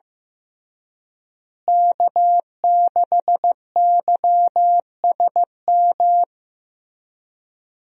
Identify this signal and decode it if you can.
Morse code, heard as a keyed tone, about 15 wpm, 700 Hz. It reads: K6YSM
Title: USA Call Signs - 5 Characters - Repeated 3x - 15wpm